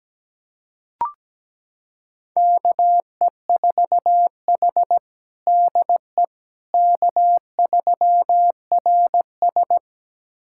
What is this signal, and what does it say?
Morse code, heard as a keyed tone, about 17 wpm, 700 Hz. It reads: KE4H DE K3RS
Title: USA Callsign DE Callsign - 4 Characters - 17wpm